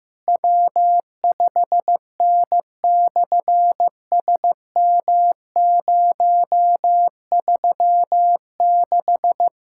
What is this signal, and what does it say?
Morse code, heard as a keyed tone, about 15 wpm, 700 Hz. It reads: W5N/SM036
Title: SOTA QSO Practice 15wpm